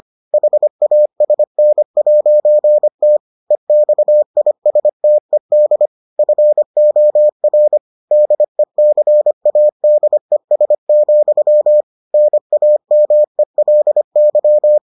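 Morse code, heard as a keyed tone, about 25 wpm, 600 Hz. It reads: HASN'T EXISTED FOR DECADES, NAMELY
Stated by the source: Morse edition of the Foundations of Amateur Radio episodes.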